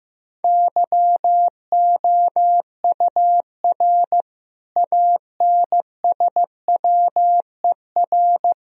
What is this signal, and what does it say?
Morse code, heard as a keyed tone, about 16 wpm, 700 Hz. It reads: YOUR ANSWER
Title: Sentences from Top 1000 Words - Review Part 3 - 15wpm